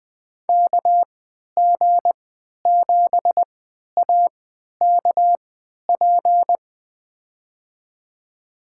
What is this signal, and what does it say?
Morse code, heard as a keyed tone, about 20 wpm, spaced as for 11 wpm, 700 Hz. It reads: KG7AKP
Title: USA Call Signs - 3x Character Spacing - 20wpm